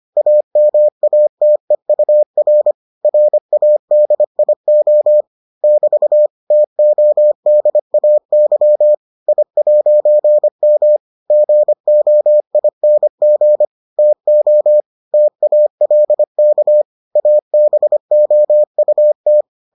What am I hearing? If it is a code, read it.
Morse code, heard as a keyed tone, about 25 wpm, 600 Hz. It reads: AMATEUR RADIO <BT> TODAY I'M GOING TO TALK ABOUT